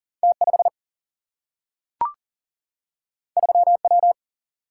Morse code, heard as a keyed tone, about 40 wpm, 700 Hz. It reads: T5 3W
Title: Single-Letter Single-Number 40wpm